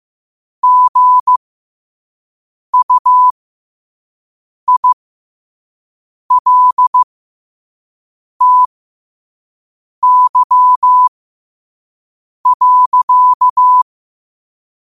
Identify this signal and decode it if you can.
Morse code, heard as a keyed tone, about 15 wpm, spaced as for 5 wpm, 1000 Hz. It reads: GUILTY.
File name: qotdfiles / 5wpm